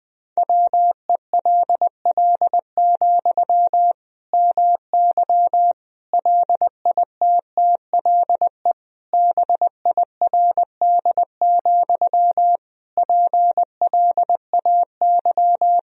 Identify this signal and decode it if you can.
Morse code, heard as a keyed tone, about 20 wpm, 700 Hz. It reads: WELL, MY LITTLE BIRD, PLAY
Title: Sentences from Top 1000 Words - Review Part 2 - 20wpm